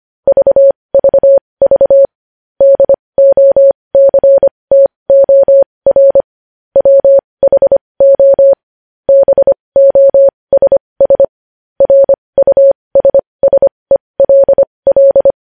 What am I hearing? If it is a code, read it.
Morse code, heard as a keyed tone, about 25 wpm, spaced as for 20 wpm, 550 Hz. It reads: VVV DOCTOR WHO BOSS RUSSELL
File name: podcast / News headlines-1x-25x20